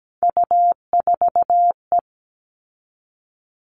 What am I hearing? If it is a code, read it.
Morse code, heard as a keyed tone, about 17 wpm, 700 Hz. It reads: U4E
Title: All Cut Numbers - 3-Digit 17wpm